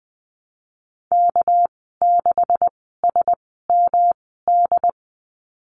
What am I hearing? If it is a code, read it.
Morse code, heard as a keyed tone, about 20 wpm, spaced as for 15 wpm, 700 Hz. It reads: K6SMD
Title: 200 Lesson 32g - Letter k - Call Signs 20wpm